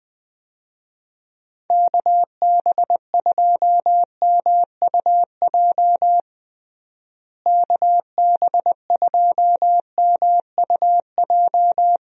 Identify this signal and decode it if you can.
Morse code, heard as a keyed tone, about 20 wpm, 700 Hz. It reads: KB2MUJ KB2MUJ
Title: USA Call Signs - 6 Characters - Repeated 3x - 20wpm